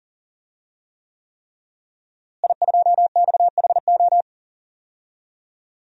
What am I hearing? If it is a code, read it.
Morse code, heard as a keyed tone, about 40 wpm, 700 Hz. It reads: I2XHO
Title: DXCC-100 Call Signs - 5 Characters - Repeated 3x - 40wpm